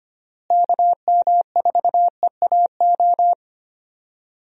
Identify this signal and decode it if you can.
Morse code, heard as a keyed tone, about 25 wpm, 700 Hz. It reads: KM4EAO
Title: USA Call Signs - 6 Characters - Repeated 3x - 25wpm